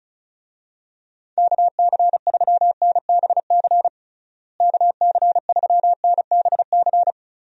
Morse code, heard as a keyed tone, about 36 wpm, 700 Hz. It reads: KC3NBC KC3NBC
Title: USA Call Signs - 6 Characters - Repeated 2x - 35wpm